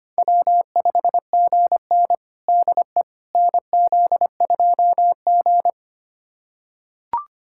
Morse code, heard as a keyed tone, about 25 wpm, 700 Hz. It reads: W5GN DE NZ2G
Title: USA Callsign DE Callsign - 4 Characters - 25wpm